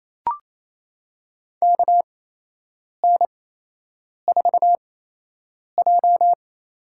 Morse code, heard as a keyed tone, about 27 wpm, spaced as for 8 wpm, 700 Hz. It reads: KN4J
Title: USA Call Signs - 6 Characters - 8x Character Spacing - 28wpm